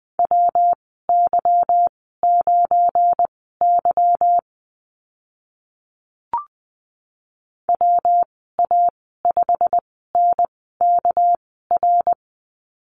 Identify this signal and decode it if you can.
Morse code, heard as a keyed tone, about 20 wpm, spaced as for 15 wpm, 700 Hz. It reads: WY9Y WA5NKR
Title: 219 Lesson 35g - Number 0 - Call Signs 20wpm